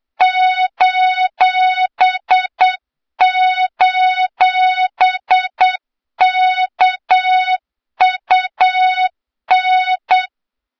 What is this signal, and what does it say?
Morse code, heard as a keyed tone, about 8 wpm, 3700 Hz, with noise. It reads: ::KUN
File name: Morze-HA2MN / hang / 23 GYAKORLAT